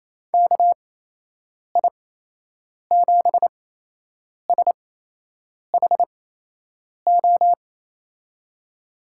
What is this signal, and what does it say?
Morse code, heard as a keyed tone, about 28 wpm, spaced as for 8 wpm, 700 Hz. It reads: KI7SHO
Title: USA Call Signs - 6 Characters - 8x Character Spacing - 28wpm